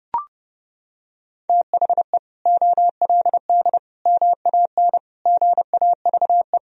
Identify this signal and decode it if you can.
Morse code, heard as a keyed tone, about 30 wpm, 700 Hz. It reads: THE OLD MAN GAVE
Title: Sentences from Top 500 Words and Abbreviations 30wpm